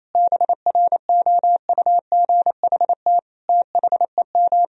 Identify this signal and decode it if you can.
Morse code, heard as a keyed tone, about 28 wpm, 700 Hz. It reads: BROUGHT THEM
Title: Sentences from Top 1000 Words - Review Part 3 - 28wpm